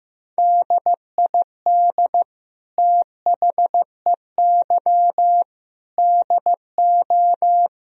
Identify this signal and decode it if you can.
Morse code, heard as a keyed tone, about 15 wpm, 700 Hz. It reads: DID THEY DO
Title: Sentences from Top 1000 Words - Review Part 4 - 15wpm